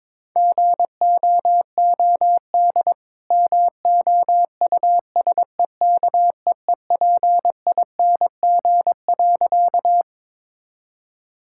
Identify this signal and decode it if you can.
Morse code, heard as a keyed tone, about 22 wpm, 690 Hz. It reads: GOOD MOUSEKEEPING.
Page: morse-code-ninja-practice-22wpm